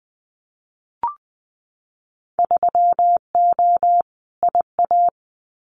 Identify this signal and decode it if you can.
Morse code, heard as a keyed tone, about 20 wpm, 700 Hz. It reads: 3O IA
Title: Winter Field Day Exchanges 20wpm